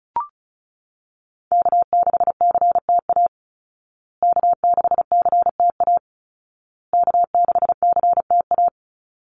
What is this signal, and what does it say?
Morse code, heard as a keyed tone, about 34 wpm, 700 Hz. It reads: K6CTA K6CTA K6CTA
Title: Call Signs Repeated 2x - 35wpm